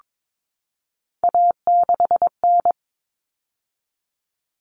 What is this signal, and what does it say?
Morse code, heard as a keyed tone, about 22 wpm, 700 Hz. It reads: A6N
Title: All Cut Numbers - 3-Digit 22wpm